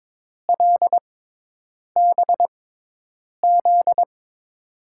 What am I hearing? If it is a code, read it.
Morse code, heard as a keyed tone, about 22 wpm, spaced as for 8 wpm, 700 Hz. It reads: LBZ
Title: USA Call Signs - 6 Characters - 6x Character Spacing - 22wpm